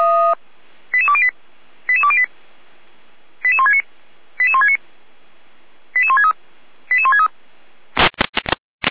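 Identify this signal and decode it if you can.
Five-tone selective call (ZVEI), heard as quick runs of five tones, tones of about 70 ms. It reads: 88208 88208 88169 88169 88163 88163
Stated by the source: ZVEI five-tone call 88303 received on an SDR.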